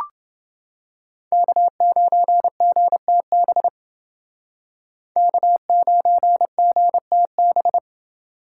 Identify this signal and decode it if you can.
Morse code, heard as a keyed tone, about 29 wpm, 700 Hz. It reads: K9GTB K9GTB